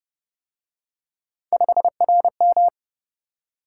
Morse code, heard as a keyed tone, about 31 wpm, 700 Hz. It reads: 5RM
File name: Speed-Racing - Three Letter-Number Combinations - 25wpm